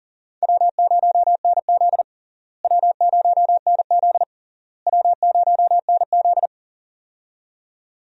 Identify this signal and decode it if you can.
Morse code, heard as a keyed tone, about 37 wpm, 700 Hz. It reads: W0NZ W0NZ W0NZ
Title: USA Call Signs - 4 Characters - Repeated 3x - 40wpm